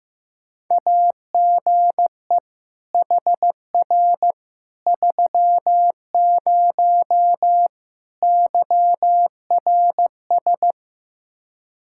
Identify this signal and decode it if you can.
Morse code, heard as a keyed tone, about 15 wpm, 700 Hz. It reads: AGE HR 30 YRS
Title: Practice QSO - Exchange 3 - More Info - 15wpm